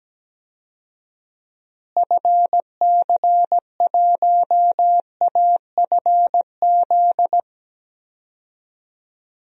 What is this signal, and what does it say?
Morse code, heard as a keyed tone, about 17 wpm, 700 Hz. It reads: FC1AFZ
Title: DXCC-100 Call Signs - 6 Characters - Repeated 2x - 17wpm